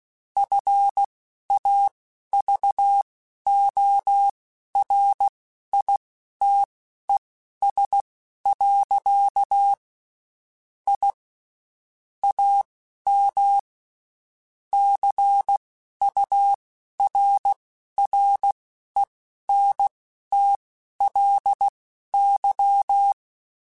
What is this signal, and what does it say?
Morse code, heard as a keyed tone, about 16 wpm, spaced as for 11 wpm, 800 Hz. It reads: FAVORITES. I AM CURRENTLY
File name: QSOs / QSOs-mp3s / WD0FBT 1@15